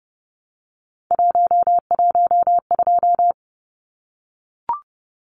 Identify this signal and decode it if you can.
Morse code, heard as a keyed tone, about 30 wpm, 700 Hz. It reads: 112
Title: Numbers - 3-Digit 30wpm